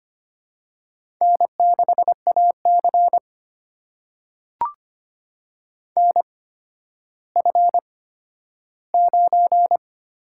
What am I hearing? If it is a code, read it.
Morse code, heard as a keyed tone, about 25 wpm, spaced as for 12 wpm, 700 Hz. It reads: N6AC NF9
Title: USA Call Signs - 4 Characters - 8x Character Spacing - 25wpm